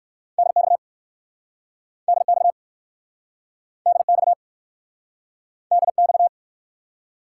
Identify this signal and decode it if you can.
Morse code, heard as a keyed tone, about 59 wpm, 700 Hz. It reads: DX DX DX DX